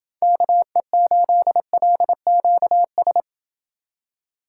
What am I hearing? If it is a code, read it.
Morse code, heard as a keyed tone, about 27 wpm, 700 Hz. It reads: KE8LQS